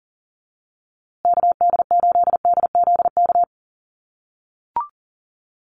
Morse code, heard as a keyed tone, about 40 wpm, 700 Hz. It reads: KD8DZK
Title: USA Call Signs Repeated 2x 40wpm